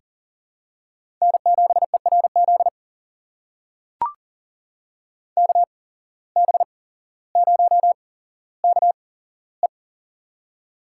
Morse code, heard as a keyed tone, about 40 wpm, spaced as for 17 wpm, 700 Hz. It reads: N7ERZ KB0KE
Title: USA Call Signs - 5 Characters - 8x Character Spacing - 40wpm